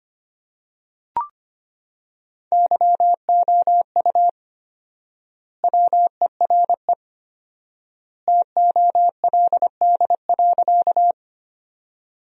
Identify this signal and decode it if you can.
Morse code, heard as a keyed tone, about 25 wpm, 700 Hz. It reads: YOU WERE TOLD.
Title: 4x Word-Spacing - Sentences from Top 400 Words - 25wpm